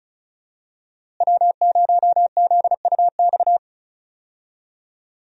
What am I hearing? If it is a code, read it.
Morse code, heard as a keyed tone, about 34 wpm, 700 Hz. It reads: W0ZUX
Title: USA Call Signs - 5 Characters - Repeated 2x - 35wpm